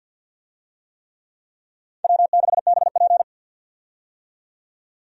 Morse code, heard as a keyed tone, about 51 wpm, 700 Hz. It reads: W6BP
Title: USA Call Signs - 4 Characters - 50wpm